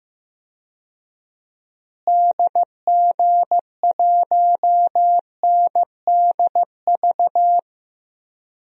Morse code, heard as a keyed tone, about 15 wpm, 700 Hz. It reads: DG1NDV
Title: DXCC-100 Call Signs - 6 Characters - Repeated 3x - 15wpm